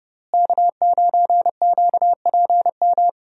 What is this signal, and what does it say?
Morse code, heard as a keyed tone, about 30 wpm, 700 Hz. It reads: K9QPM
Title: USA Call Signs - 5 Characters - Repeated 2x - 30wpm